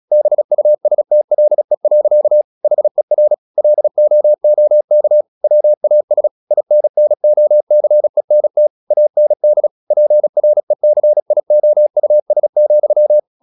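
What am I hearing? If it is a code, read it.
Morse code, heard as a keyed tone, about 36 wpm, 600 Hz. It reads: BUSTLE. HER LOOK WAS INNOCENT AND PRECIOUS,
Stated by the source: Morse practice at 36 wpm, text of Ken Follett's novel Never, from CW med Gnister.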